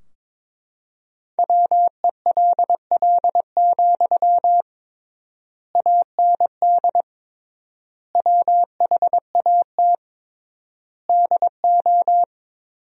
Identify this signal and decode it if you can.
Morse code, heard as a keyed tone, about 22 wpm, 700 Hz. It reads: WELL, AND WHAT DO
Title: 3x Word-Spacing - Sentences from Top 200 Words - 22wpm